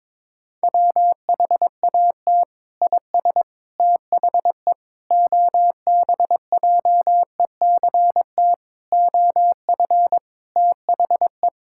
Morse code, heard as a keyed tone, about 22 wpm, 700 Hz. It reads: WHAT IS THE OBJECT OF THE
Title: Sentences from Top 1000 Words - Review Part 5 - 22wpm